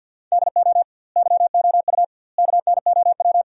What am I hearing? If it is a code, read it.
Morse code, heard as a keyed tone, about 50 wpm, 700 Hz. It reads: DO YOU KNOW
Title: 2x Word-Spacing - Sentences from Top 100 Words - 50wpm